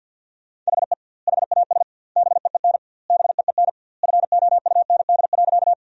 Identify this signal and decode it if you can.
Morse code, heard as a keyed tone, about 51 wpm, 700 Hz. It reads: HE HAS BEEN BEEN FOUND.
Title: 2x Word-Spacing - Sentences from Top 300 Words - 50wpm